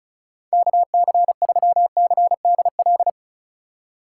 Morse code, heard as a keyed tone, about 36 wpm, 700 Hz. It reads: KC3CDL
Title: USA Call Signs - 3x Character Spacing - 35wpm